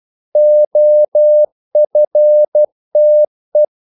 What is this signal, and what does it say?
Morse code, heard as a keyed tone, about 12 wpm, 600 Hz. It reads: OFTE
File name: Korte ord DK 12wpm